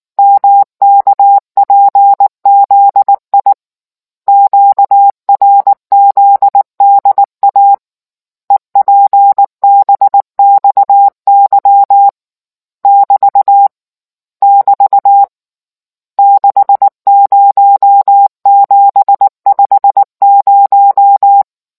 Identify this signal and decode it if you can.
Morse code, heard as a keyed tone, about 19 wpm, 800 Hz. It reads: MKPZI QRZDA EPBXY <BT> <BT> 60750